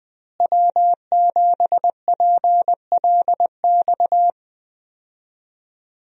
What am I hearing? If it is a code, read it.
Morse code, heard as a keyed tone, about 20 wpm, 700 Hz. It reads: W7PLX
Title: USA Call Signs - 5 Characters - Repeated 3x - 20wpm